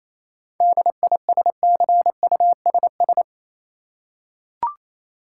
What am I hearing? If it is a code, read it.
Morse code, heard as a keyed tone, about 28 wpm, 700 Hz. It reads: DISCUSS
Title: Seven Letter Words - 28wpm